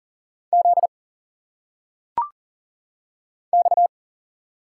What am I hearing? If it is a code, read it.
Morse code, heard as a keyed tone, about 40 wpm, 700 Hz. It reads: Z X
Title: Single Letters - Phonetic Pronunciation 40wpm